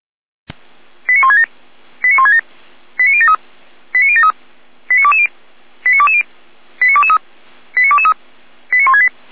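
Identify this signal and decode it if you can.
Five-tone selective call (ZVEI), heard as quick runs of five tones, tones of about 80 ms. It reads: 89167 89167 89073 89073 89220 89220 89223 89223 89168
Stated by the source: ZVEI five-tone selective call received on an SDR.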